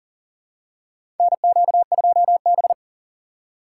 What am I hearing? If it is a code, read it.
Morse code, heard as a keyed tone, about 40 wpm, 700 Hz. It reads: NQ2B
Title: USA Call Signs - 4 Characters - 6x Character Spacing - 40wpm